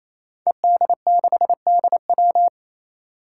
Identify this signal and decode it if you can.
Morse code, heard as a keyed tone, about 29 wpm, 700 Hz. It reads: ED6DW
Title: Five Challenging Letter-Number Combinations 28wpm